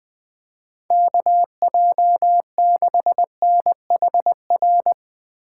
Morse code, heard as a keyed tone, about 19 wpm, 700 Hz. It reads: KJ6NHR
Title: Speed Racing - USA 2x3 Call Signs 17wpm